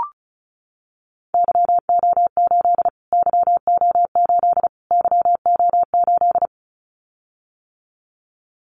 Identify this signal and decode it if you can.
Morse code, heard as a keyed tone, about 34 wpm, 700 Hz. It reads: YO8 YO8 YO8
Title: Callsign Prefixes for Top 25 DXCC - Repeated 3x - 35wpm